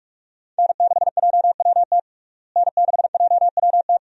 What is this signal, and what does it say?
Morse code, heard as a keyed tone, about 41 wpm, 700 Hz. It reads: N6JWT N6JWT
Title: USA Call Signs - 5 Characters - Repeated 2x - 45wpm